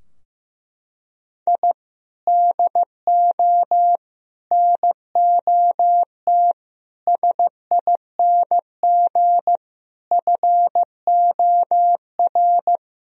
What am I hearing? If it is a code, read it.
Morse code, heard as a keyed tone, about 16 wpm, 700 Hz. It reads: I DO NOT SING FOR
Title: Sentences from Top 1000 Words - Review Part 2 - 15wpm